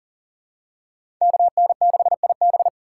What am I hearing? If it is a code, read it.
Morse code, heard as a keyed tone, about 37 wpm, 700 Hz. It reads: KN6IB
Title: USA Call Signs - 5 Characters - 4x Character Spacing - 40wpm